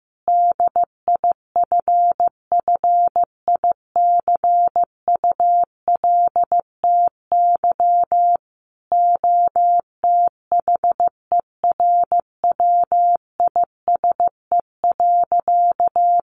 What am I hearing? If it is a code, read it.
Morse code, heard as a keyed tone, about 15 wpm, 700 Hz. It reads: DIFFICULTY OTHERWISE.
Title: Sentences from Top 1500 Words 15wpm